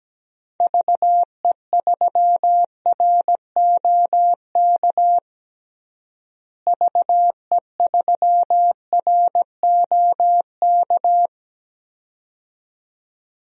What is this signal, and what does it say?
Morse code, heard as a keyed tone, about 17 wpm, 700 Hz. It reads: VE3ROK VE3ROK
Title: DXCC-100 Call Signs - 6 Characters - Repeated 2x - 17wpm